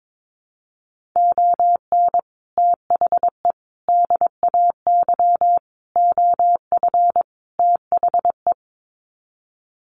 Morse code, heard as a keyed tone, about 22 wpm, 700 Hz. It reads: ON THE DAY OF THE
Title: Sets of 5 Words Encore 22wpm